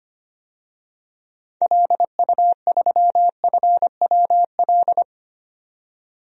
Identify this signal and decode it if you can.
Morse code, heard as a keyed tone, about 25 wpm, 700 Hz. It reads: LU3FWL
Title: DXCC-100 Call Signs - 6 Characters - 25wpm